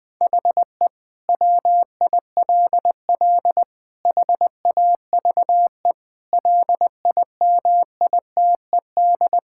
Morse code, heard as a keyed tone, about 20 wpm, 700 Hz. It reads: HE WILL HAVE LIMITED
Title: Sentences from Top 1000 Words - Review Part 6 - 20wpm